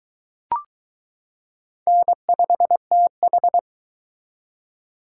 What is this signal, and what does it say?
Morse code, heard as a keyed tone, about 22 wpm, 700 Hz. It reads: N5TH